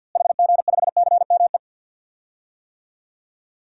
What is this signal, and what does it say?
Morse code, heard as a keyed tone, about 51 wpm, 700 Hz. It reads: HG5CME
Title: DXCC-100 Call Signs - 6 Characters - Repeated 3x - 50wpm